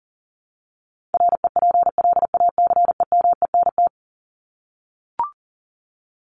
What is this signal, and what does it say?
Morse code, heard as a keyed tone, about 40 wpm, 700 Hz. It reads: REPLACEMENT
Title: Top 1751-2000 Words 40wpm